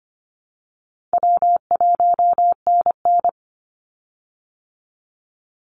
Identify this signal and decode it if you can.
Morse code, heard as a keyed tone, about 25 wpm, 700 Hz. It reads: W1NN
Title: Call Signs Repeated 3x - 25wpm